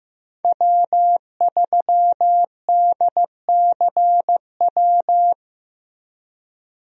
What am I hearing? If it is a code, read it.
Morse code, heard as a keyed tone, about 15 wpm, 700 Hz. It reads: W3DCW
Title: USA Call Signs - 5 Characters - Repeated 3x - 15wpm